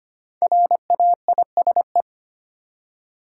Morse code, heard as a keyed tone, about 25 wpm, 700 Hz. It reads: RAISE
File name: 5 Letter Words - First 3 Letters Given - 25wpm